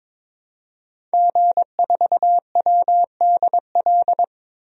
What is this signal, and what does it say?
Morse code, heard as a keyed tone, about 22 wpm, 700 Hz. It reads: G4WDL
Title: DXCC-100 Call Signs - 5 Characters - 22wpm